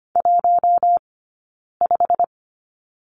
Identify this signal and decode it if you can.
Morse code, heard as a keyed tone, about 24 wpm, spaced as for 9 wpm, 700 Hz. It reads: 15
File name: Lesson 15d - Number 5 - Two Characters 25wpm